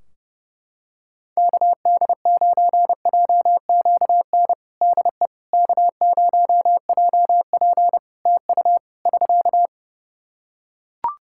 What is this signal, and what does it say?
Morse code, heard as a keyed tone, about 30 wpm, 700 Hz. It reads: KD9JQN DE K0JP TU <SK>
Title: Practice QSO - Exchange 4 - Ending - 30wpm